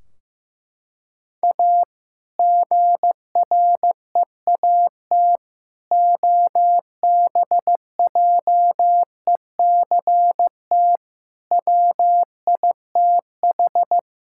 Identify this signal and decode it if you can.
Morse code, heard as a keyed tone, about 16 wpm, 700 Hz. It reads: A GREAT OBJECT WITH
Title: Sentences from Top 1600 Words 15wpm